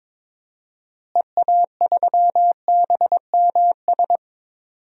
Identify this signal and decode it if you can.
Morse code, heard as a keyed tone, about 21 wpm, 700 Hz. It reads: EA3BMS
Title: DXCC-100 Call Signs - 6 Characters - Repeated 2x - 22wpm